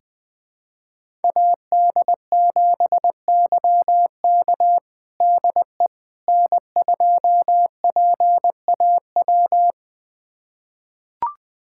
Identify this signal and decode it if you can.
Morse code, heard as a keyed tone, about 20 wpm, 700 Hz. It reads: AD7YK DE N2PAW
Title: USA Callsign DE Callsign - 5 Characters - 20wpm